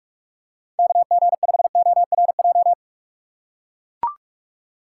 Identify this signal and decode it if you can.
Morse code, heard as a keyed tone, about 44 wpm, 700 Hz. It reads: KG5ORJ
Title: USA Call Signs - 6 Characters - 6x Character Spacing - 45wpm